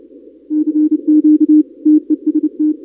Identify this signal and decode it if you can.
Morse code, heard as a keyed tone, about 29 wpm, 300 Hz. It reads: CQ TEST
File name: contests / 2019-cw-wpx-kt5j-k5pi- -2330 0-31